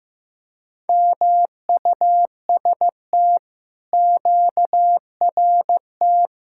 Morse code, heard as a keyed tone, about 15 wpm, 700 Hz. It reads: MUST QRT
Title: Speed-Racing Common QSO Phrases Encore - 15wpm